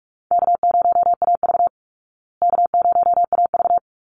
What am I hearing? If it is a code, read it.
Morse code, heard as a keyed tone, about 44 wpm, 700 Hz. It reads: K0AV K0AV
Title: Call Signs Repeated 2x - 45wpm